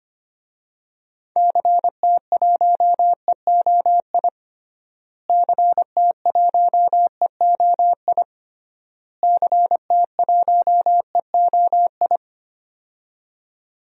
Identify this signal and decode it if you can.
Morse code, heard as a keyed tone, about 25 wpm, 700 Hz. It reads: CT1EOI CT1EOI CT1EOI
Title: DXCC-100 Call Signs - 6 Characters - Repeated 3x - 25wpm